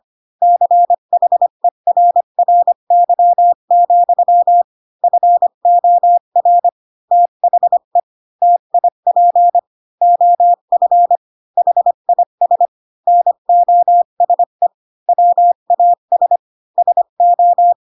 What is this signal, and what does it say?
Morse code, heard as a keyed tone, about 25 wpm, 700 Hz. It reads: CHERRY, FOR THE TIP OF HIS NOSE WAS SO
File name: Pinocchio / Pinocchio ch1 20 25 ik1hge